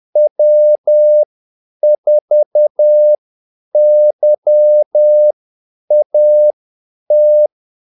Morse code, heard as a keyed tone, about 10 wpm, spaced as for 8 wpm, 600 Hz. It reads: W4YAT
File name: Callsigns-10-8wpm